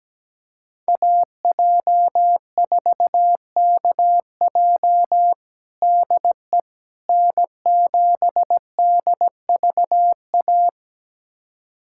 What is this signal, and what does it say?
Morse code, heard as a keyed tone, about 17 wpm, 700 Hz. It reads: AJ4KJ DE N7DVA
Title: USA Callsign DE Callsign - 5 Characters - 17wpm